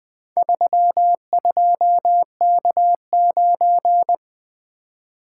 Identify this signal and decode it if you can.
Morse code, heard as a keyed tone, about 20 wpm, 700 Hz. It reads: 32K9
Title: Four Letter-Number Combinations - 20wpm